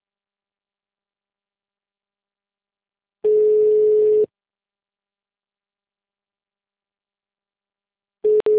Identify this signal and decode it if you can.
Ringback tone (Europe, 425 Hz).